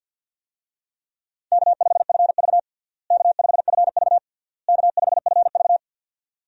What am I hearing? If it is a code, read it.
Morse code, heard as a keyed tone, about 51 wpm, 700 Hz. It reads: K5FV K5FV K5FV
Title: USA Call Signs - 4 Characters - Repeated 3x - 50wpm